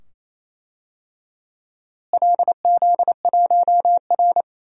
Morse code, heard as a keyed tone, about 28 wpm, 700 Hz. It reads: LZ1R
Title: DXCC-100 Call Signs - 4 Characters - Repeated 2x - 28wpm